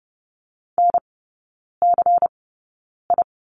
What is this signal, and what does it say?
Morse code, heard as a keyed tone, about 30 wpm, spaced as for 9 wpm, 700 Hz. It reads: NCI
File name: Lesson 17e - Letter u - Three Characters 30wpm